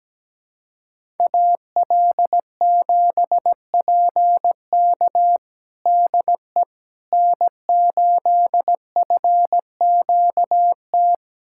Morse code, heard as a keyed tone, about 17 wpm, 700 Hz. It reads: AL7PK DE N8FQT
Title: USA Callsign DE Callsign - 5 Characters - 17wpm